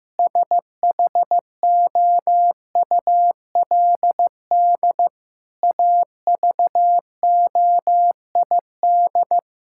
Morse code, heard as a keyed tone, about 15 wpm, 700 Hz. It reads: SHOULD AVOID
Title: Sentences from Top 1000 Words - Review Part 6 - 15wpm